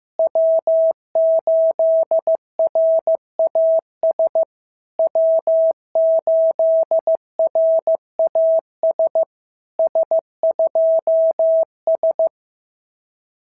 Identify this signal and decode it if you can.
Morse code, heard as a keyed tone, about 15 wpm, 650 Hz. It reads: W8RAS W8RAS S2S
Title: SOTA QSO Practice 15wpm